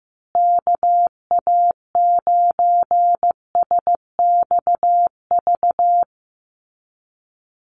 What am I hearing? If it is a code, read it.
Morse code, heard as a keyed tone, about 15 wpm, 700 Hz. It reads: KA9SXV
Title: Call Signs Repeated 3x - 15wpm